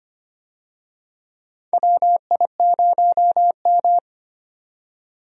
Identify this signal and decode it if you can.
Morse code, heard as a keyed tone, about 25 wpm, 700 Hz. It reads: WI0M
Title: USA Call Signs - 4 Characters - 25wpm